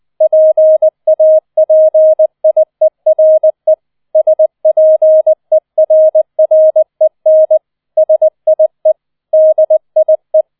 Morse code, heard as a keyed tone, about 20 wpm, 630 Hz. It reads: PAPIERE SPERREN SIE DIE